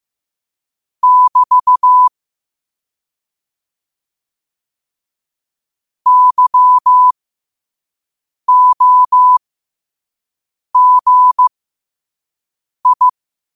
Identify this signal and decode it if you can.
Morse code, heard as a keyed tone, about 15 wpm, spaced as for 5 wpm, 1000 Hz. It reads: <BT> YOGI
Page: Quotes for Thu, 14 Aug 2025 in Morse Code at 5 words per minute.